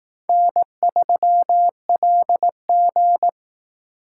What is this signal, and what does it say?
Morse code, heard as a keyed tone, about 18 wpm, 700 Hz. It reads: N3LG
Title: Speed Racing - USA 1x2 Call Signs 15wpm